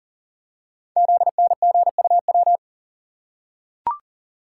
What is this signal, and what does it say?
Morse code, heard as a keyed tone, about 42 wpm, 690 Hz. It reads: ZNGUW
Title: Code Groups - 5 Letters - 40wpm